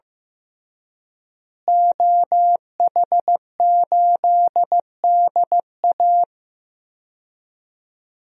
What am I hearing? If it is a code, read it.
Morse code, heard as a keyed tone, about 15 wpm, 700 Hz. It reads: OH8DA
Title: DXCC-100 Call Signs - 5 Characters - Repeated 2x - 15wpm